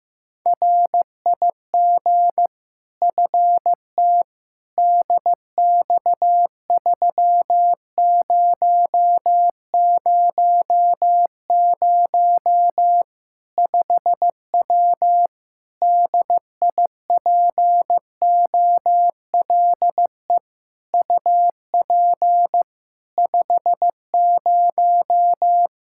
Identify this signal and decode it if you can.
Morse code, heard as a keyed tone, about 15 wpm, 700 Hz. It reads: RIG FT DX3000 5W DIPOLE UP 50
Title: Practice QSO - Exchange 2 - Info - 15wpm